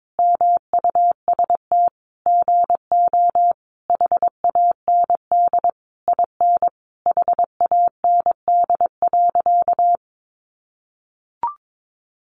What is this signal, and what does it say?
Morse code, heard as a keyed tone, about 22 wpm, 700 Hz. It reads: MUST GO HAND IN HAND.
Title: Sentences from Top 1200 Words 22wpm